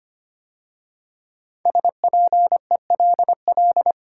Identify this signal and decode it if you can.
Morse code, heard as a keyed tone, about 25 wpm, 700 Hz. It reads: SPELL
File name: 5 Letter Words - First 3 Letters Given - 25wpm